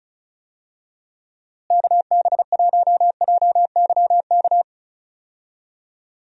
Morse code, heard as a keyed tone, about 36 wpm, 700 Hz. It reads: KB1JYK
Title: USA Call Signs - 6 Characters - Repeated 2x - 35wpm